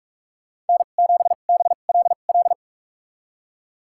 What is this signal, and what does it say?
Morse code, heard as a keyed tone, about 46 wpm, spaced as for 32 wpm, 700 Hz. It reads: N7BLL
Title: USA Call Signs - 5 Characters - 2x Character Spacing - 45wpm